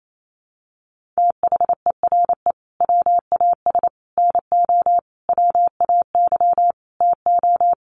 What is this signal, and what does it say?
Morse code, heard as a keyed tone, about 28 wpm, 700 Hz. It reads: THERE WAS NO WAY TO
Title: Sets of 5 Words Encore 28wpm